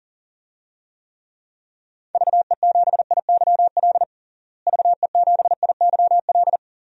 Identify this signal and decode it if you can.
Morse code, heard as a keyed tone, about 37 wpm, 700 Hz. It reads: VE7IYL VE7IYL
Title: DXCC-100 Call Signs - 6 Characters - Repeated 3x - 40wpm